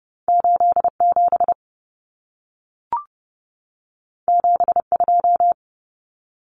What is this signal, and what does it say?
Morse code, heard as a keyed tone, about 30 wpm, 700 Hz. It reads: 87 72
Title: Numbers - 2-Digit 30wpm